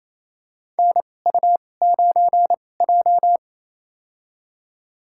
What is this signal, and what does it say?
Morse code, heard as a keyed tone, about 28 wpm, spaced as for 20 wpm, 700 Hz. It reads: NU9J
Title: USA Call Signs - 2x Character Spacing - 28wpm